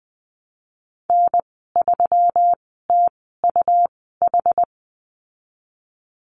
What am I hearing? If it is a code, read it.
Morse code, heard as a keyed tone, about 20 wpm, spaced as for 15 wpm, 700 Hz. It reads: N3TUH
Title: 161 Lesson 26g - Letter g - Call Signs 20wpm